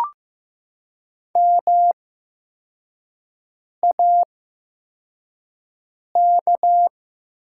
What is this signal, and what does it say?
Morse code, heard as a keyed tone, about 15 wpm, spaced as for 4 wpm, 700 Hz. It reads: MAK
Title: 8x Character Spacing - Top 200 Words - 15wpm